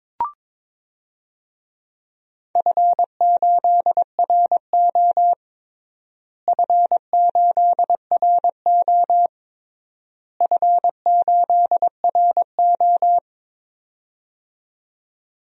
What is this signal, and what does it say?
Morse code, heard as a keyed tone, about 22 wpm, 700 Hz. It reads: F8RO F8RO F8RO
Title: DXCC-100 Call Signs - 4 Characters - Repeated 3x - 22wpm